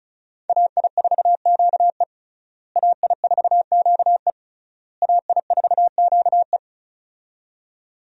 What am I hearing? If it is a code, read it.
Morse code, heard as a keyed tone, about 34 wpm, 700 Hz. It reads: AI4QE AI4QE AI4QE
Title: USA Call Signs - 5 Characters - Repeated 3x - 35wpm